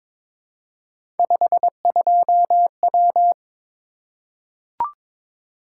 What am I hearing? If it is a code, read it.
Morse code, heard as a keyed tone, about 22 wpm, 700 Hz. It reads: 52W
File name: practice / 22wpm / Three Letter-Number Combinations - 22wpm